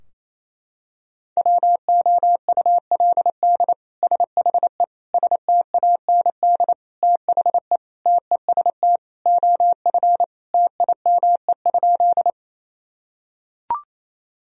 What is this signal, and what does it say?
Morse code, heard as a keyed tone, about 27 wpm, 700 Hz. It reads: WOULD SHE STAND THE TEST OF TIME?
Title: Sentences from Top 1000 Words - Review Part 3 - 28wpm